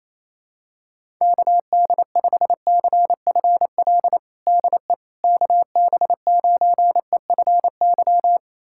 Morse code, heard as a keyed tone, about 28 wpm, 700 Hz. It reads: KD5CFL DE KB9EFY
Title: USA Callsign DE Callsign 28wpm